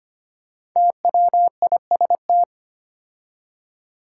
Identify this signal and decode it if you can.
Morse code, heard as a keyed tone, about 25 wpm, 700 Hz. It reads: TWIST
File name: 5 Letter Words - First 3 Letters Given - 25wpm